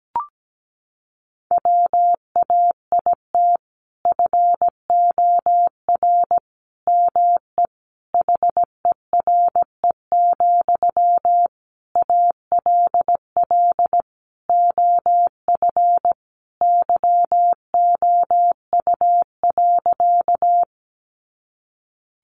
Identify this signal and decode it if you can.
Morse code, heard as a keyed tone, about 17 wpm, 700 Hz. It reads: WAIT FOR ME HERE, ALL OF YOU.
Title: Sentences from Top 500 Words 17wpm